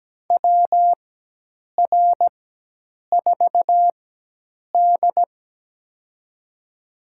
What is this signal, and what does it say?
Morse code, heard as a keyed tone, about 17 wpm, spaced as for 8 wpm, 700 Hz. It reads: WR4D
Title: USA Call Signs - 4 Characters - 4x Character Spacing - 17wpm